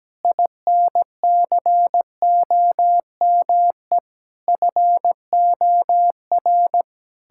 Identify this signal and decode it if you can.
Morse code, heard as a keyed tone, about 17 wpm, 700 Hz. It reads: INCOME FOR
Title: Sentences from Top 1000 Words - Review Part 5 - 17wpm